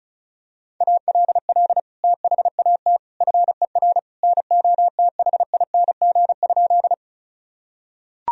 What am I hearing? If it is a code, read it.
Morse code, heard as a keyed tone, about 36 wpm, 700 Hz. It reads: ALL THAT FER NOTHING?
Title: Sentences from Top 500 Words and Abbreviations 35wpm